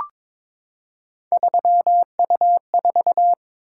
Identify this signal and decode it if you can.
Morse code, heard as a keyed tone, about 22 wpm, 700 Hz. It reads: 3U4
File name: practice / 22wpm / Three Letter-Number Combinations - 22wpm